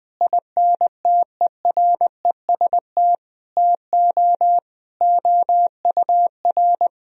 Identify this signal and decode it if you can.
Morse code, heard as a keyed tone, about 20 wpm, 700 Hz. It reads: INTEREST TO OUR
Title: Sentences from Top 1000 Words - Review Part 4 - 20wpm